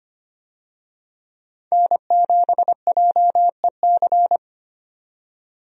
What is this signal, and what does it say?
Morse code, heard as a keyed tone, about 25 wpm, 700 Hz. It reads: N7JEC
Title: USA Call Signs - 5 Characters - 25wpm